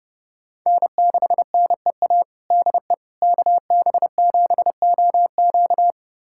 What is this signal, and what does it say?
Morse code, heard as a keyed tone, about 32 wpm, 700 Hz. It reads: N6NEA DE KB7OQ
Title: USA Callsign DE Callsign - 5 Characters - 30wpm